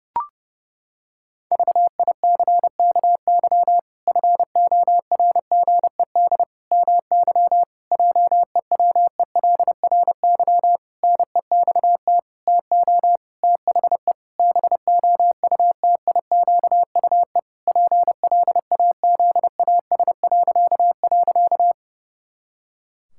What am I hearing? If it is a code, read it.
Morse code, heard as a keyed tone, about 30 wpm, 700 Hz. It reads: VICKY FORGED MY JEWELRY NEXT TO THE BOUTIQUE PLAZAS..
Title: Pangram Challenge v2 - 30wpm